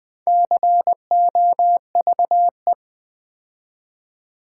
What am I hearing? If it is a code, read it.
Morse code, heard as a keyed tone, about 20 wpm, 700 Hz. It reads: COVE
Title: Build A Word - 2 Letters At A Time - 8-Letter Words - 20wpm